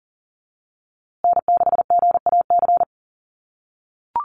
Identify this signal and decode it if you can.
Morse code, heard as a keyed tone, about 40 wpm, 700 Hz. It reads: N6GAC
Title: USA Call Signs Repeated 3x 40wpm